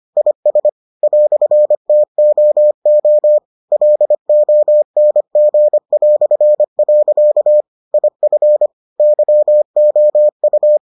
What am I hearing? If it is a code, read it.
Morse code, heard as a keyed tone, about 25 wpm, 600 Hz. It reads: IS "TOO LONG". IF YOU